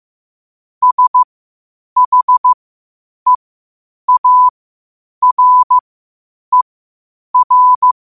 Morse code, heard as a keyed tone, about 15 wpm, spaced as for 8 wpm, 1000 Hz. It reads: SHEARER
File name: qotdfiles / 8wpm